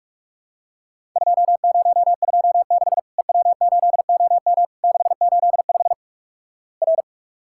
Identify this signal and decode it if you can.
Morse code, heard as a keyed tone, about 47 wpm, 700 Hz. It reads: 2026 EW8OM 685 R
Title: Worked All Europe DX Contest - QTC Exchange - 45wpm